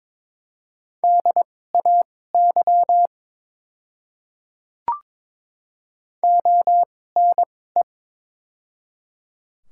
Morse code, heard as a keyed tone, about 22 wpm, spaced as for 16 wpm, 700 Hz. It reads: DAY ONE
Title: 2x Character Spacing - Top 100 Words - 22wpm